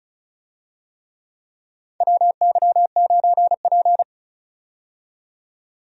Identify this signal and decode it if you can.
Morse code, heard as a keyed tone, about 34 wpm, 700 Hz. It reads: WY9P
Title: USA Call Signs - 4 Characters - 35wpm